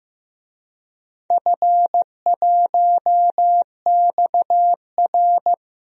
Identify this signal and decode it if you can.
Morse code, heard as a keyed tone, about 15 wpm, 700 Hz. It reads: F1XR
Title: DXCC-100 Call Signs - 4 Characters - 15wpm